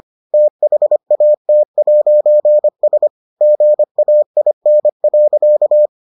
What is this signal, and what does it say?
Morse code, heard as a keyed tone, about 25 wpm, 600 Hz. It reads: THAT'S GAIN.